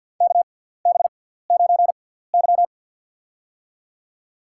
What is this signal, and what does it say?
Morse code, heard as a keyed tone, about 50 wpm, spaced as for 17 wpm, 690 Hz. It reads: KB9Y
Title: USA Call Signs - 6x Character Spacing - 50wpm